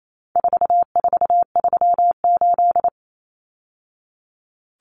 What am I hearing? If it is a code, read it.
Morse code, heard as a keyed tone, about 28 wpm, 700 Hz. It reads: 4438
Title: Numbers - 4-Digit 28wpm